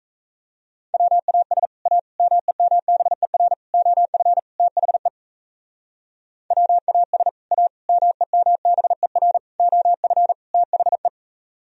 Morse code, heard as a keyed tone, about 36 wpm, 700 Hz. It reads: WAS A MEMBER OF THE WAS A MEMBER OF THE